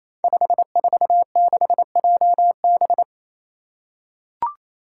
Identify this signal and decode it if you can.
Morse code, heard as a keyed tone, about 27 wpm, 700 Hz. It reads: 546JB
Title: Five Challenging Letter-Number Combinations 28wpm